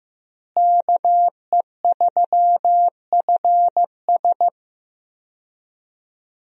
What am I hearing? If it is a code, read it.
Morse code, heard as a keyed tone, about 15 wpm, 700 Hz. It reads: KE3FS
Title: USA Call Signs - 5 Characters - Repeated 2x - 15wpm